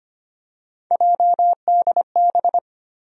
Morse code, heard as a keyed tone, about 25 wpm, 700 Hz. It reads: JDB
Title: USA Call Sign Postfixes - 3 Letters 25wpm